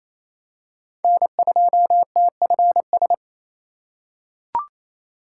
Morse code, heard as a keyed tone, about 28 wpm, 700 Hz. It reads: N2TFS
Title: USA Call Signs - 5 Characters - 2x Character Spacing - 28wpm